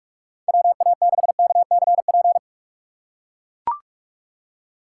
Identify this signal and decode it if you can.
Morse code, heard as a keyed tone, about 41 wpm, 700 Hz. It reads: WA6KCP
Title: USA Call Signs - 6x Character Spacing - 45wpm